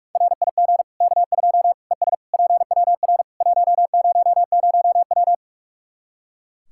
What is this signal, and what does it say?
Morse code, heard as a keyed tone, about 44 wpm, 700 Hz. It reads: RIG K2 ES PWR 100W
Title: K1USN SST QSO Practice 45wpm